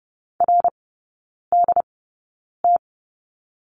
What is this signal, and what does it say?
Morse code, heard as a keyed tone, about 30 wpm, spaced as for 9 wpm, 700 Hz. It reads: RDT
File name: Lesson 17e - Letter u - Three Characters 30wpm